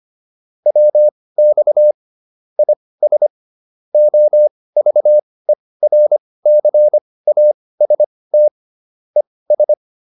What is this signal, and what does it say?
Morse code, heard as a keyed tone, about 25 wpm, spaced as for 18 wpm, 600 Hz. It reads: WX IS OVERCAST ES